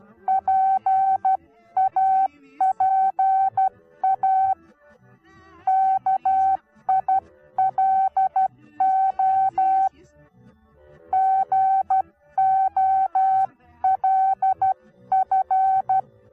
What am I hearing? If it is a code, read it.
Morse code, heard as a keyed tone, about 12 wpm, 770 Hz. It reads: PAPA KILO GOLF
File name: Onlinelehrgang / morsen / DJ4UF Morsen19